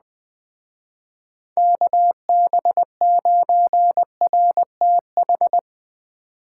Morse code, heard as a keyed tone, about 19 wpm, 700 Hz. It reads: KB9RTH
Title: Speed Racing - USA 2x3 Call Signs 17wpm